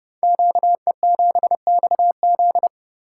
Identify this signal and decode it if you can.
Morse code, heard as a keyed tone, about 30 wpm, 700 Hz. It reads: QE7XZ
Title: Code Groups - 5 Letter:Numbers - 30wpm